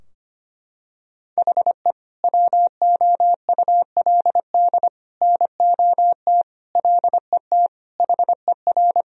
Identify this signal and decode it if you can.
Morse code, heard as a keyed tone, about 25 wpm, 700 Hz. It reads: HE WOULD NOT LET HER
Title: Sentences from Top 1000 Words - Review Part 2 - 25wpm